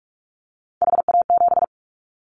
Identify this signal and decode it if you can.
Morse code, heard as a keyed tone, about 44 wpm, 700 Hz. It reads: HA7
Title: Callsign Prefixes for Top 25 DXCC - 45wpm